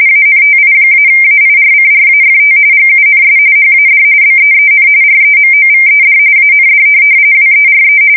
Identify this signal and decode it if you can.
RTTY, about 45.45 baud. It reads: ANTI-AIRCRAFT GUNS HAD BEEN REMOVED AND WERE
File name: fsk170 / USS-New-Jersey